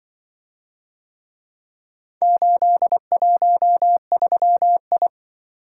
Morse code, heard as a keyed tone, about 25 wpm, 700 Hz. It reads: 813I
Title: Speed-Racing - Four Letter-Number Combinations - 20wpm